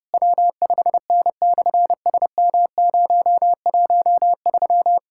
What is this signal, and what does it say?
Morse code, heard as a keyed tone, about 30 wpm, 700 Hz. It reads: W5N/SM013
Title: SOTA QSO Practice 30wpm